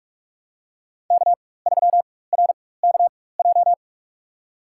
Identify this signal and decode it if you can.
Morse code, heard as a keyed tone, about 47 wpm, spaced as for 21 wpm, 690 Hz. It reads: K3RKJ
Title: USA Call Signs - 5 Characters - 4x Character Spacing - 45wpm